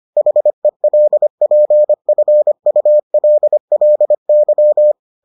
Morse code, heard as a keyed tone, about 25 wpm, 600 Hz. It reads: HELPFULLY